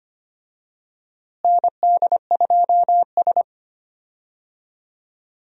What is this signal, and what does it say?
Morse code, heard as a keyed tone, about 25 wpm, 700 Hz. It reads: ND2S
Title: USA Call Signs - 4 Characters - Repeated 3x - 25wpm